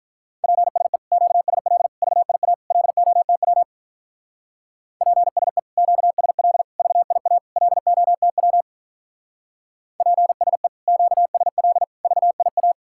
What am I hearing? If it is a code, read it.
Morse code, heard as a keyed tone, about 45 wpm, 700 Hz. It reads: PSE QSL VIA LOTW PSE QSL VIA LOTW PSE QSL VIA